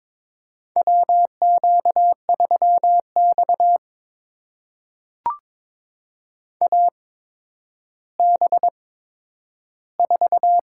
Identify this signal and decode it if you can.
Morse code, heard as a keyed tone, about 22 wpm, spaced as for 11 wpm, 700 Hz. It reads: WQ3X AB4
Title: USA Call Signs - 8x Character Spacing - 22wpm